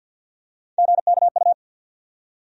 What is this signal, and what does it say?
Morse code, heard as a keyed tone, about 52 wpm, 700 Hz. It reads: GKU
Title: USA Call Sign Postfixes - 3 Letters 50wpm